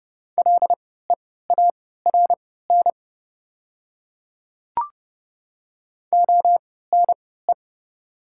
Morse code, heard as a keyed tone, about 30 wpm, spaced as for 17 wpm, 700 Hz. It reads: LEARN ONE
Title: 3x Character Spacing - Top 300 Words - 30wpm